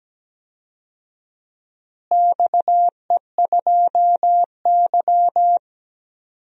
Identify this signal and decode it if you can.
Morse code, heard as a keyed tone, about 17 wpm, 700 Hz. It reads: XE2Y
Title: DXCC-100 Call Signs - 4 Characters - Repeated 2x - 17wpm